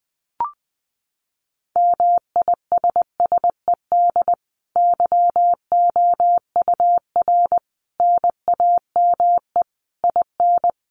Morse code, heard as a keyed tone, about 20 wpm, 700 Hz. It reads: MISSED YOUR NAME IN
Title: Common QSO Phrases 20wpm